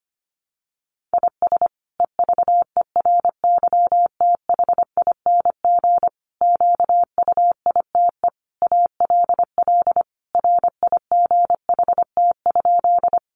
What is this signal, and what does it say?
Morse code, heard as a keyed tone, about 25 wpm, 700 Hz. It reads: IS EVERYTHING QUITE ALL RIGHT?
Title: Sentences from Top 700 Words 25wpm